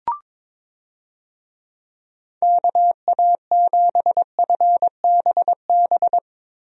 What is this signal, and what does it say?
Morse code, heard as a keyed tone, about 22 wpm, 700 Hz. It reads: KA7FBB
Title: USA Call Signs - 6 Characters - Repeated 2x - 22wpm